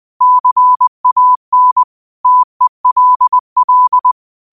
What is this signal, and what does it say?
Morse code, heard as a keyed tone, about 20 wpm, 1000 Hz. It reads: CAN TELL